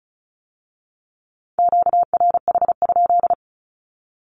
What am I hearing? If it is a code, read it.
Morse code, heard as a keyed tone, about 34 wpm, 700 Hz. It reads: QRH?
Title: Q-Codes with Spoken Meaning - 35wpm